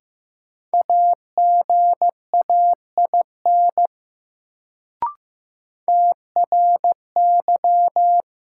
Morse code, heard as a keyed tone, about 15 wpm, 700 Hz. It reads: AGAIN TRY
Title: Sentences from Top 200 Words - New Word Drill - 15wpm